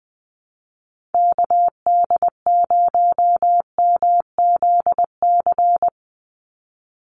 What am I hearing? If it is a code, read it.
Morse code, heard as a keyed tone, about 20 wpm, 700 Hz. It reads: KD0MZC
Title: USA Call Signs 20wpm